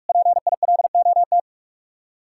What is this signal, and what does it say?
Morse code, heard as a keyed tone, about 56 wpm, 700 Hz. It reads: PILOT
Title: Speed-Racing Common QSO Phrases Encore - 45wpm